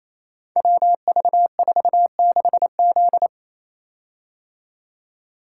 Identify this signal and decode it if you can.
Morse code, heard as a keyed tone, about 28 wpm, 700 Hz. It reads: WV46Z
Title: Code Groups - 5 Letter:Numbers - 28wpm